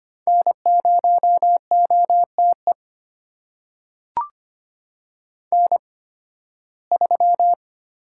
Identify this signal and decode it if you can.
Morse code, heard as a keyed tone, about 25 wpm, spaced as for 16 wpm, 700 Hz. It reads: N0OTE N3
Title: USA Call Signs - 5 Characters - 8x Character Spacing - 25wpm